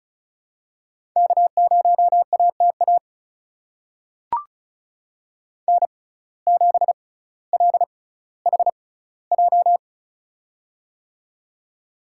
Morse code, heard as a keyed tone, about 36 wpm, spaced as for 18 wpm, 700 Hz. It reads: K0ATA N7LHJ
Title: USA Call Signs - 5 Characters - 6x Character Spacing - 35wpm